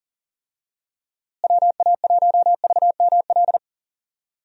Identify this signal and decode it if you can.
Morse code, heard as a keyed tone, about 40 wpm, 700 Hz. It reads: WA1VML
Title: USA Call Signs - 6 Characters - Repeated 3x - 40wpm